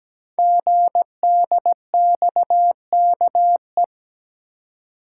Morse code, heard as a keyed tone, about 17 wpm, 700 Hz. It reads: GDXKE
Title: Code Groups - 5 Letters - 17wpm